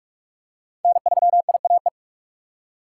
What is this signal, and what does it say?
Morse code, heard as a keyed tone, about 42 wpm, 700 Hz. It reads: N3IAE
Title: USA Call Signs - 5 Characters - 3x Character Spacing - 45wpm